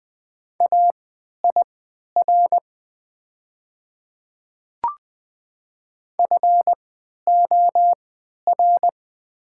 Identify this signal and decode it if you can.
Morse code, heard as a keyed tone, about 20 wpm, spaced as for 11 wpm, 700 Hz. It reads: AIR FOR
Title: 3x Character Spacing - Top 200 Words - 20wpm